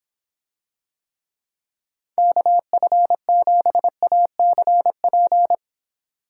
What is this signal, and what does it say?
Morse code, heard as a keyed tone, about 26 wpm, 700 Hz. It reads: KF7ACP